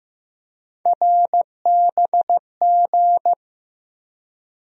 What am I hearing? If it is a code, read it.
Morse code, heard as a keyed tone, about 15 wpm, 700 Hz. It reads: RBG
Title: USA Call Sign Postfixes - 3 Letters 15wpm